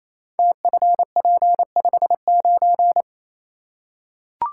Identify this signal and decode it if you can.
Morse code, heard as a keyed tone, about 29 wpm, 700 Hz. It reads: TFP59
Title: Code Groups - 5 Letter:Numbers - 28wpm